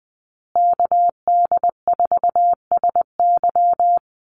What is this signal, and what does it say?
Morse code, heard as a keyed tone, about 20 wpm, 700 Hz. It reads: KD4SY
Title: USA Call Signs Repeated 3x 20wpm